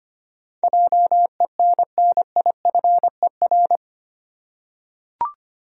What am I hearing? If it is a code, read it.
Morse code, heard as a keyed tone, about 25 wpm, 700 Hz. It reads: JENNIFER
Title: USA Common Names - 25wpm